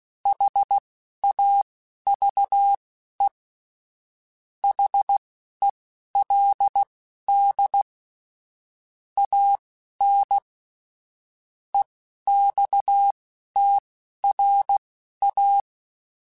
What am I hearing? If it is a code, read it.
Morse code, heard as a keyed tone, about 16 wpm, spaced as for 11 wpm, 800 Hz. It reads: HAVE HELD AN EXTRA
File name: QSOs / QSOs-mp3s / WA7QOX 2@10